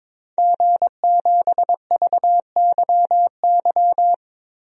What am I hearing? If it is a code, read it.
Morse code, heard as a keyed tone, about 22 wpm, 700 Hz. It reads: G7VYY
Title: DXCC-100 Call Signs - 5 Characters - Repeated 2x - 22wpm